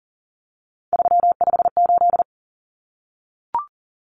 Morse code, heard as a keyed tone, about 40 wpm, 700 Hz. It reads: 358
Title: Common Cut Numbers - 3-Digit 40wpm